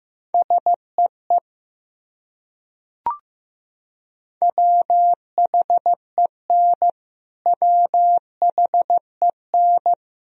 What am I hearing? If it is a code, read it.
Morse code, heard as a keyed tone, about 15 wpm, 700 Hz. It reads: SEE WHEN WHEN
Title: Top 100 Words Repeated 15wpm